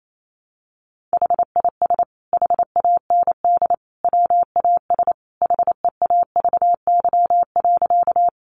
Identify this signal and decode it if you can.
Morse code, heard as a keyed tone, about 28 wpm, 700 Hz. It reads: HIS HAND WAS HEAVY.
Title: Sentences from Top 1100 Words 28wpm